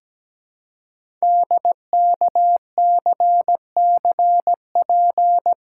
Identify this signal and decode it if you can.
Morse code, heard as a keyed tone, about 17 wpm, 700 Hz. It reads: DKCCP
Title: Code Groups - 5 Letters - 17wpm